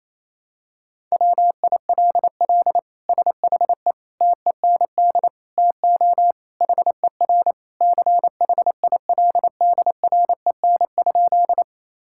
Morse code, heard as a keyed tone, about 28 wpm, 700 Hz. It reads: WILL SHE TEND TO HER CHILDREN?
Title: Sentences from Top 1000 Words - Review Part 5 - 28wpm